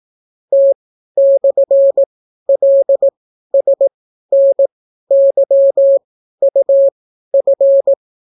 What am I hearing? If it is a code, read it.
Morse code, heard as a keyed tone, about 18 wpm, spaced as for 12 wpm, 550 Hz. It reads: T/LSNYUF